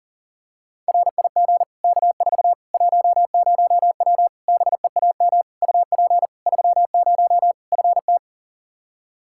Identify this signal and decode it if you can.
Morse code, heard as a keyed tone, about 40 wpm, 700 Hz. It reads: RIG K4 10W BEAM UP 30 FT
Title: K1USN SST QSO Practice 40wpm